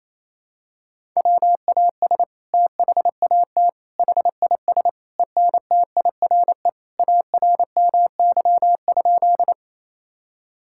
Morse code, heard as a keyed tone, about 28 wpm, 700 Hz. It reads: WAS THAT HIS ENTIRE ARMY?
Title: Sentences from Top 1000 Words - Review Part 5 - 28wpm